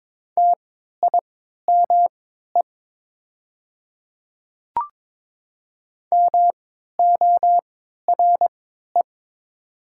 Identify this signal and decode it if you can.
Morse code, heard as a keyed tone, about 22 wpm, spaced as for 12 wpm, 700 Hz. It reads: TIME MORE
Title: 3x Character Spacing - Top 100 Words - 22wpm